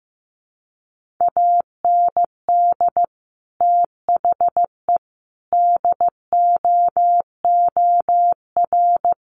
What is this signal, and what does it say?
Morse code, heard as a keyed tone, about 15 wpm, 700 Hz. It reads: AND THE DOOR
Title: Sentences from Top 1400 Words 15wpm